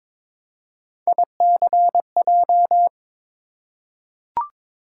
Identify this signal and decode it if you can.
Morse code, heard as a keyed tone, about 22 wpm, 700 Hz. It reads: ICJ
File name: practice / 22wpm / Three Letter-Number Combinations - 22wpm